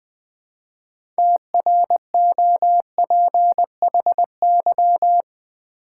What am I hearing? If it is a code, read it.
Morse code, heard as a keyed tone, about 20 wpm, 700 Hz. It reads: TROPHY
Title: 6 Letter Words - First 3 Letters Given - 20wpm